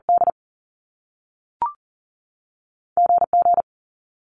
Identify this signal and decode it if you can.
Morse code, heard as a keyed tone, about 42 wpm, spaced as for 3 wpm, 700 Hz. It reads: DG G
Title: Top Two Letter Combinations in Words 40wpm